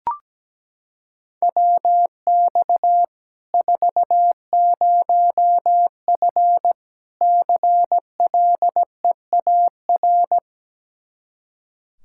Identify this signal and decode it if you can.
Morse code, heard as a keyed tone, about 17 wpm, 700 Hz. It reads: WX 40F CLEAR
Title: Practice QSO - Exchange 2 - Info - 17wpm